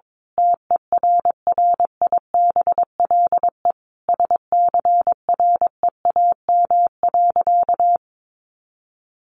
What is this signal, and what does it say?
Morse code, heard as a keyed tone, about 22 wpm, 700 Hz. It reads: TERRIBLE SCREAM.
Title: A Sound Of Thunder 22wpm